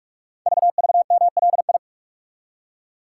Morse code, heard as a keyed tone, about 44 wpm, 700 Hz. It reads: VVMLI
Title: Code Groups - 5 Letters - 45wpm